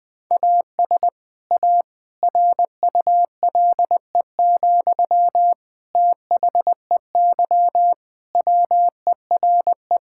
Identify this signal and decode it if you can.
Morse code, heard as a keyed tone, about 21 wpm, 700 Hz. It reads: AS A RULE, THEY WERE
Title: Sentences from Top 1000 Words - Review Part 6 - 20wpm